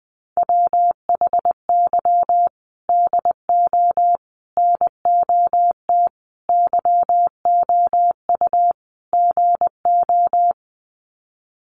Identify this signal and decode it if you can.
Morse code, heard as a keyed tone, about 20 wpm, 700 Hz. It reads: WHY DO NOT YOU GO
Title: Sets of 5 Words Encore 20wpm